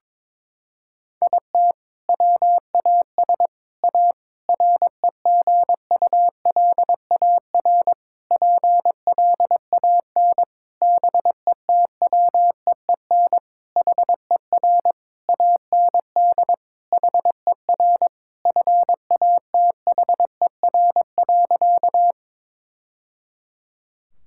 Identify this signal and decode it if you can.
Morse code, heard as a keyed tone, about 22 wpm, 700 Hz. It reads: IT WAS A REGULAR PLAN BETWEEN HER AND HER FATHER.
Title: Sentences from Top 1000 Words - Review Part 6 - 22wpm